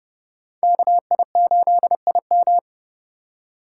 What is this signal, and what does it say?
Morse code, heard as a keyed tone, about 30 wpm, 700 Hz. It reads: KI8IM
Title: USA Call Signs - 3x Character Spacing - 30wpm